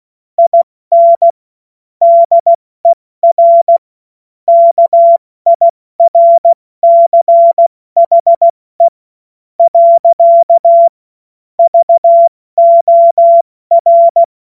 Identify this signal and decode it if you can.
Morse code, heard as a keyed tone, about 16 wpm, spaced as for 13 wpm, 680 Hz. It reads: IN DER KIRCHE. VOR
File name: Morsekurs / Witze / Witz 27. Juli 2021, 70 F80